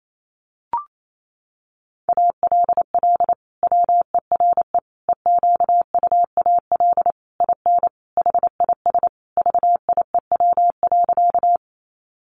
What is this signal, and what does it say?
Morse code, heard as a keyed tone, about 28 wpm, 700 Hz. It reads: ALL WERE EQUAL IN HIS VIEW.
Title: Sentences from Top 1400 Words 28wpm